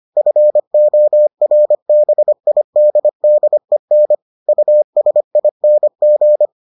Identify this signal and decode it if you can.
Morse code, heard as a keyed tone, about 25 wpm, 600 Hz. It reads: FORBIDDEN USING